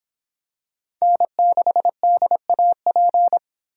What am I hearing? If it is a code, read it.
Morse code, heard as a keyed tone, about 24 wpm, 700 Hz. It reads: N6DAP